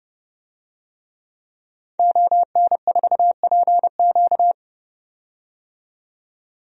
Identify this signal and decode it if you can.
Morse code, heard as a keyed tone, about 31 wpm, 700 Hz. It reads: ON4PQ
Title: DXCC-100 Call Signs - 5 Characters - 30wpm